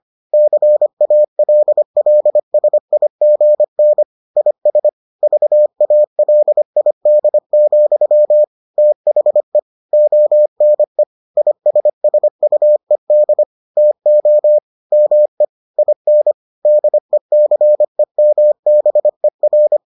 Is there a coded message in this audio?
Yes: Morse code from 0.3 to 19.8 s, heard as a keyed tone, about 25 wpm, 600 Hz. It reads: CALLSIGN IS VALID, THE ONE ISSUED TO ME IN DECEMBER